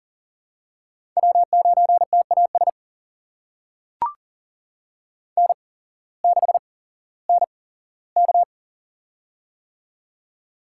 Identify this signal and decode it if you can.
Morse code, heard as a keyed tone, about 40 wpm, spaced as for 19 wpm, 700 Hz. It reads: W9TAS N6NK
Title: USA Call Signs - 8x Character Spacing - 40wpm